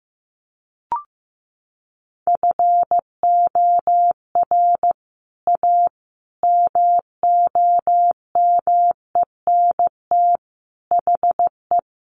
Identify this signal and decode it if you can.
Morse code, heard as a keyed tone, about 14 wpm, 700 Hz. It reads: FOR A MOMENT HE
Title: Sentences from Top 1100 Words 15wpm